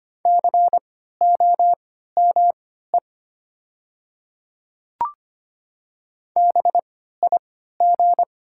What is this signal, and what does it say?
Morse code, heard as a keyed tone, about 25 wpm, spaced as for 14 wpm, 700 Hz. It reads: COME BIG
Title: 3x Character Spacing - Top 500 Words - 25wpm